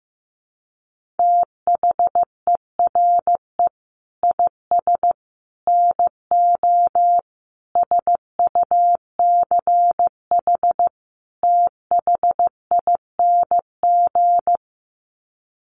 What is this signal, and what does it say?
Morse code, heard as a keyed tone, about 15 wpm, 700 Hz. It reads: THERE IS NO SUCH THING
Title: Sets of 5 Words Encore 15wpm